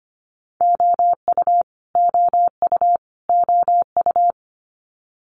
Morse code, heard as a keyed tone, about 25 wpm, 700 Hz. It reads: OU OU OU
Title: Multi-letter Phonemes Repeated 3x 25wpm